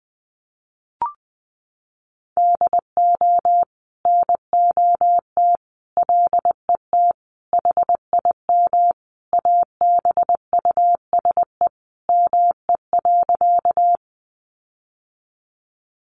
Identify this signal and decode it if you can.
Morse code, heard as a keyed tone, about 20 wpm, 700 Hz. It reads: DO NOT LET HIM ABUSE ME.
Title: Sentences from Top 1300 Words 20wpm